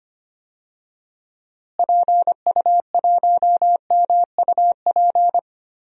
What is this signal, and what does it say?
Morse code, heard as a keyed tone, about 25 wpm, 700 Hz. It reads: PU1MUP
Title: DXCC-100 Call Signs - 6 Characters - Repeated 2x - 25wpm